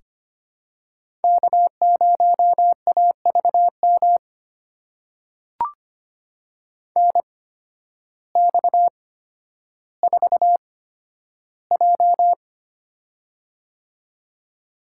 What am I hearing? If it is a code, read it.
Morse code, heard as a keyed tone, about 25 wpm, spaced as for 12 wpm, 700 Hz. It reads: K0AVM NX4J
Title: USA Call Signs - 8x Character Spacing - 25wpm